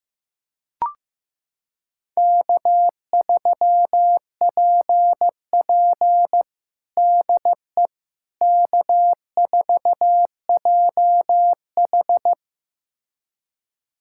Morse code, heard as a keyed tone, about 15 wpm, 700 Hz. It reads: K3PP DE K4JH
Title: USA Callsign DE Callsign - 4 Characters - 15wpm